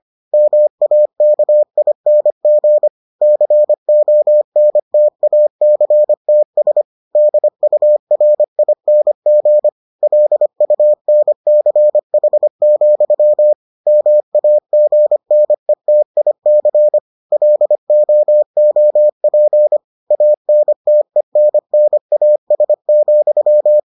Morse code, heard as a keyed tone, about 25 wpm, 600 Hz. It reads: MAKING CONTACTS DURING LUNCH, MAGNETIC LOOP ANTENNAS,